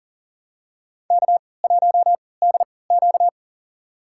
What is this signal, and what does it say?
Morse code, heard as a keyed tone, about 40 wpm, spaced as for 23 wpm, 700 Hz. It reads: K1DQ
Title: USA Call Signs - 4 Characters - 3x Character Spacing - 40wpm